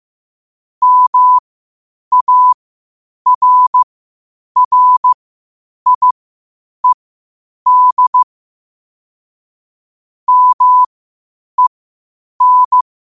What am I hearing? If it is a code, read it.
Morse code, heard as a keyed tone, about 15 wpm, spaced as for 8 wpm, 1000 Hz. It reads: MARRIED MEN
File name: qotdfiles / 8wpm